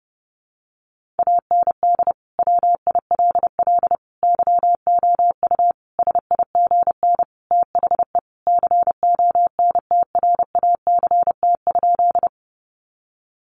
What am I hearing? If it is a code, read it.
Morse code, heard as a keyed tone, about 30 wpm, 700 Hz. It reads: AND WILL YOU SIGN THE CONTRACT?
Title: Sentences from Top 1400 Words 30wpm